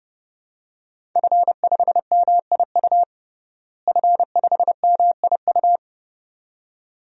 Morse code, heard as a keyed tone, about 30 wpm, 700 Hz. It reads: F5MIU F5MIU
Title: DXCC-100 Call Signs - 5 Characters - Repeated 2x - 30wpm